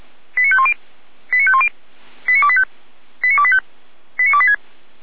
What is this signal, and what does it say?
Five-tone selective call (ZVEI), heard as quick runs of five tones, tones of about 70 ms. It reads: 89510 89510 89285 89285 89286